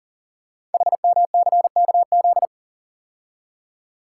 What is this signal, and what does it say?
Morse code, heard as a keyed tone, about 40 wpm, 700 Hz. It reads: HMCKZ
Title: Code Groups - 5 Letters - 40wpm